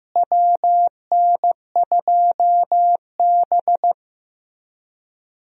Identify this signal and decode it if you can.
Morse code, heard as a keyed tone, about 15 wpm, 700 Hz. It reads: WN2B
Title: USA Call Signs - 4 Characters - Repeated 3x - 15wpm